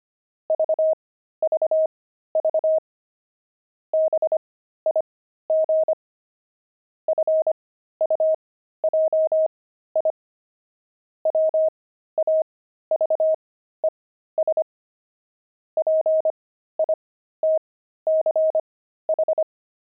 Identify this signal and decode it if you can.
Morse code, heard as a keyed tone, about 25 wpm, spaced as for 13 wpm, 630 Hz. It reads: VVV BIG FUJI WAVES PITCH
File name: pangram-fs-13-cs-25-fuji